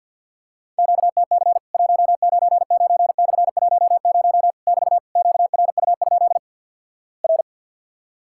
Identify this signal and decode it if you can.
Morse code, heard as a keyed tone, about 50 wpm, 700 Hz. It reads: QTC 199/10 <BT> QRU? R
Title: Worked All Europe DX Contest - QTC Exchange with Repeats - 50wpm